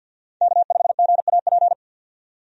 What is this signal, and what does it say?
Morse code, heard as a keyed tone, about 52 wpm, 700 Hz. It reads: K5GAP
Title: USA Call Signs - 5 Characters - 50wpm